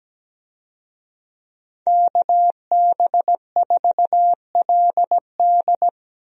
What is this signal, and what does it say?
Morse code, heard as a keyed tone, about 17 wpm, 700 Hz. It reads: KB4LD
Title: USA Call Signs - 5 Characters - Repeated 2x - 17wpm